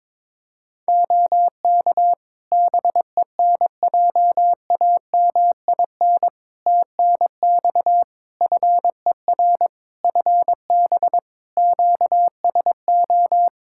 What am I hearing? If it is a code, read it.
Morse code, heard as a keyed tone, about 22 wpm, 700 Hz. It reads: OK BENJAMIN TNX FER FB QSO
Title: K1USN SST QSO Practice 22wpm